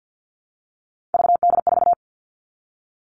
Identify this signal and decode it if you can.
Morse code, heard as a keyed tone, about 50 wpm, 700 Hz. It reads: VD4
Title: Three Challenging Letter-Number Combinations 50wpm